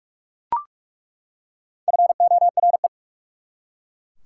Morse code, heard as a keyed tone, about 46 wpm, 690 Hz. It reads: FORE
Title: Prefixes and Suffixes 45wpm